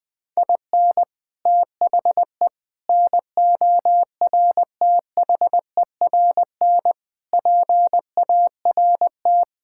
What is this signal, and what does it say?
Morse code, heard as a keyed tone, about 20 wpm, 700 Hz. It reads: IN THE NORTHERN PART
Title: Sets of 6 Words - 20wpm